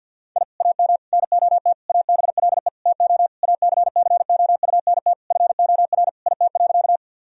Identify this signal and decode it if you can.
Morse code, heard as a keyed tone, about 50 wpm, 700 Hz. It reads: I AM NOT ABLE TO ACCOUNT FOR IT.
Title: Sentences from Top 1000 Words - Review Part 2 - 50wpm